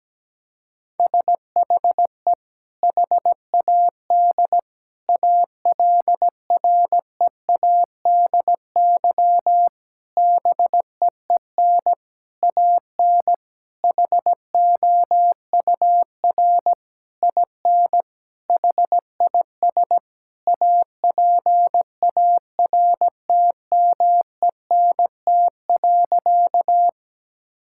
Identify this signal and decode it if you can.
Morse code, heard as a keyed tone, about 17 wpm, 700 Hz. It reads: SHE HAD ALREADY BEEN AN HOUR IN HIS APARTMENT.
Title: Sentences from Top 1000 Words - Review Part 6 - 17wpm